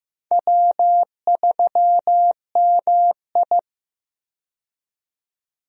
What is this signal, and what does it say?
Morse code, heard as a keyed tone, about 15 wpm, 700 Hz. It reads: W3MI
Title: USA Call Signs - 4 Characters - 15wpm